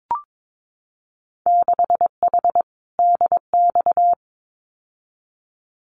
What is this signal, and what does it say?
Morse code, heard as a keyed tone, about 22 wpm, 700 Hz. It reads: 6H DX
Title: Winter Field Day Exchanges Encore 22wpm